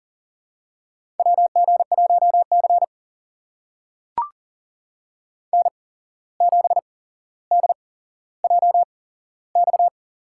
Morse code, heard as a keyed tone, about 40 wpm, spaced as for 16 wpm, 700 Hz. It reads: WG1C N7DJX
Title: USA Call Signs - 8x Character Spacing - 40wpm